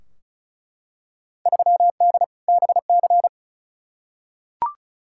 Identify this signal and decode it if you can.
Morse code, heard as a keyed tone, about 34 wpm, 700 Hz. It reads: 3D BC
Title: Field Day Exchanges Encore - 35wpm